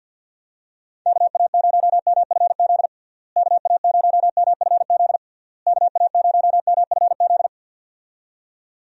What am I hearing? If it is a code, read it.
Morse code, heard as a keyed tone, about 46 wpm, 700 Hz. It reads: KA0MFZ KA0MFZ KA0MFZ
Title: USA Call Signs - 6 Characters - Repeated 3x - 50wpm